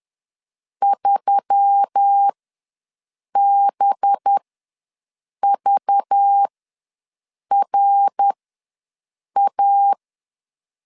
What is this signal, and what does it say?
Morse code, heard as a keyed tone, about 10 wpm, spaced as for 6 wpm, 800 Hz. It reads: 3BVRA
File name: LEZIONI / 5 LBVC / 5 LBV ws 10wpm 3645